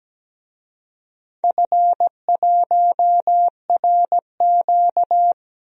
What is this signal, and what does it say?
Morse code, heard as a keyed tone, about 17 wpm, 700 Hz. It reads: F1RQ
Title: DXCC-100 Call Signs - 4 Characters - Repeated 2x - 17wpm